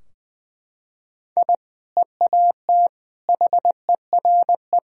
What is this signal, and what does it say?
Morse code, heard as a keyed tone, about 20 wpm, 700 Hz. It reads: I EAT HERE
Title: Sentences from Top 1000 Words - Review Part 3 - 20wpm